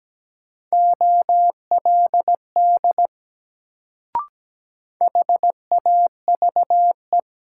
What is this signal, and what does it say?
Morse code, heard as a keyed tone, about 17 wpm, 700 Hz. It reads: OLD HAVE
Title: Sets of 2 Words - New Word Drill - 17wpm